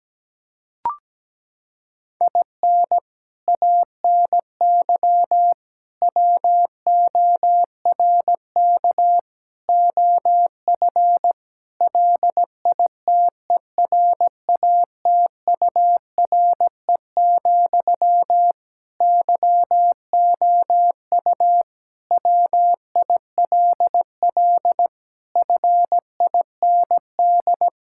Morse code, heard as a keyed tone, about 17 wpm, 700 Hz. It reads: IN ANY WORK OF LITERATURE, YOU WILL FIND
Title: Sentences from Top 1000 Words - Review Part 6 - 17wpm